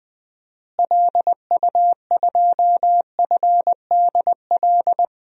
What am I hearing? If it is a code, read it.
Morse code, heard as a keyed tone, about 20 wpm, 700 Hz. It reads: LU2FDL
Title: DXCC-100 Call Signs - 6 Characters - Repeated 3x - 20wpm